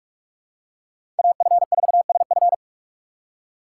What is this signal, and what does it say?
Morse code, heard as a keyed tone, about 45 wpm, 700 Hz. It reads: AF4SF
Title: USA Call Signs - 2x Character Spacing - 45wpm